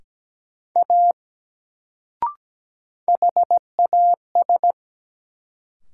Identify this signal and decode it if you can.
Morse code, heard as a keyed tone, about 17 wpm, 700 Hz. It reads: A HAS
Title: Sentences from Top 100 Words - New Word Drill - 17wpm